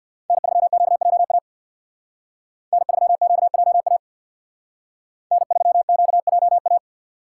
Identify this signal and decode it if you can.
Morse code, heard as a keyed tone, about 60 wpm, 700 Hz. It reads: N3QJA N3QJA N3QJA
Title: Speed Racing - USA 1x3 Call Signs 50wpm